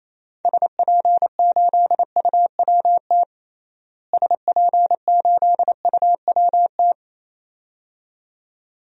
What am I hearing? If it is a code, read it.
Morse code, heard as a keyed tone, about 28 wpm, 700 Hz. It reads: SP8UWT SP8UWT
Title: DXCC-100 Call Signs - 6 Characters - Repeated 2x - 28wpm